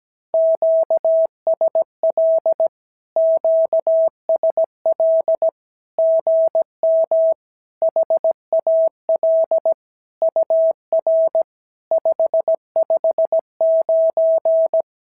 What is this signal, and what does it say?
Morse code, heard as a keyed tone, about 17 wpm, 650 Hz. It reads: QSL QSL GM HAL UR 559
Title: SOTA QSO Practice 17wpm